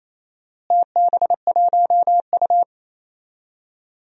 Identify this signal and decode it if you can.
Morse code, heard as a keyed tone, about 28 wpm, 700 Hz. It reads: TB1U
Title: DXCC-100 Call Signs - 4 Characters - Repeated 2x - 28wpm